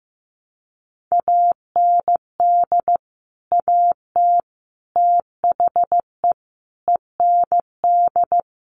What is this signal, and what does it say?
Morse code, heard as a keyed tone, about 15 wpm, 700 Hz. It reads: AND AT THE END
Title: Sets of 5 Words 15wpm